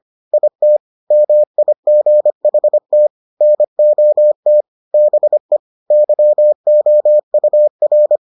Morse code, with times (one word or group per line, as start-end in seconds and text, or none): 0.3-0.8 IT
1.1-3.1 MIGHT
3.4-4.6 NOT
4.9-5.6 BE
5.9-8.2 YOUR